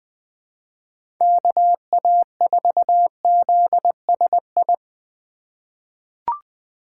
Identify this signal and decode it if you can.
Morse code, heard as a keyed tone, about 20 wpm, 700 Hz. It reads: KA4ZSI
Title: USA Call Signs - 4x Character Spacing - 20wpm